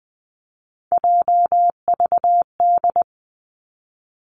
Morse code, heard as a keyed tone, about 20 wpm, 700 Hz. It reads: JVD
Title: Three Challenging Letter-Number Combinations 20wpm